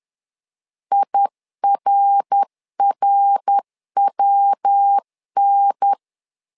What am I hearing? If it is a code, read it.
Morse code, heard as a keyed tone, about 10 wpm, 800 Hz. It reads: IRRWN
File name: LEZIONI / 3 MWRO / 3 MWR 10wpm 1952